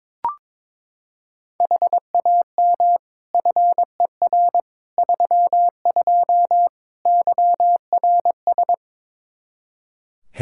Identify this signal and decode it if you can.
Morse code, heard as a keyed tone, about 22 wpm, 700 Hz. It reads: HAM FER 32 YRS
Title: K1USN SST QSO Practice 22wpm